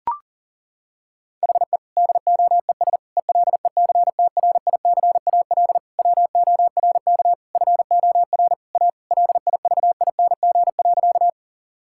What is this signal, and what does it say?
Morse code, heard as a keyed tone, about 37 wpm, 700 Hz. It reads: HE DOES ELECTRICAL WORK FOR A LIVING.
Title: Sentences from Top 1000 Words - Review Part 6 - 40wpm